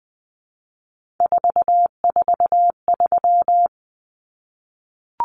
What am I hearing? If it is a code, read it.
Morse code, heard as a keyed tone, about 20 wpm, 700 Hz. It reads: 443
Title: Common Cut Numbers - 3-Digit 20wpm